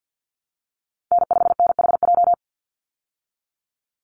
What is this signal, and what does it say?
Morse code, heard as a keyed tone, about 50 wpm, 700 Hz. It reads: N5NHJ
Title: Call Signs Repeated 3x - 50wpm